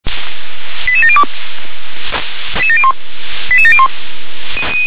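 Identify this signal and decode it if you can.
Five-tone selective call (ZVEI), heard as quick runs of five tones, tones of about 70 ms. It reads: 88702 88701
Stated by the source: ZVEI five-tone call received by SDR.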